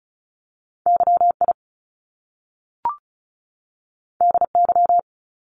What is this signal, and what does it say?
Morse code, heard as a keyed tone, about 36 wpm, 700 Hz. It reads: YI DY
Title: Top Two Letter Combinations in Words 35wpm